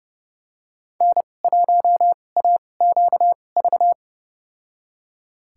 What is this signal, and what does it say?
Morse code, heard as a keyed tone, about 29 wpm, spaced as for 21 wpm, 700 Hz. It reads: N1AQV
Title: USA Call Signs - 2x Character Spacing - 30wpm